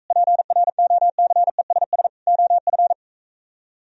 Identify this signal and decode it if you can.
Morse code, heard as a keyed tone, about 45 wpm, 700 Hz. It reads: PROCESS OF